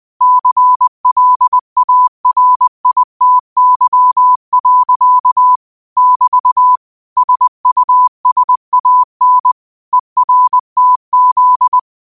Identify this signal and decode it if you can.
Morse code, heard as a keyed tone, about 20 wpm, 1000 Hz. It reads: CLARITY. <BT> SUSAN ERTZ